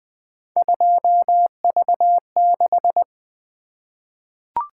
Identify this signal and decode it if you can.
Morse code, heard as a keyed tone, about 21 wpm, 700 Hz. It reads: 2V6
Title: Three Letter-Number Combinations - 20wpm